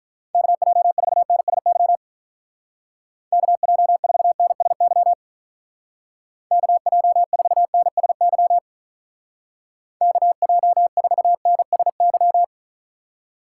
Morse code, heard as a keyed tone, about 42 wpm, 700 Hz. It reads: KJ4NSY KJ4NSY KJ4NSY KJ4NSY
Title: Speed Racing - USA Call Signs 35wpm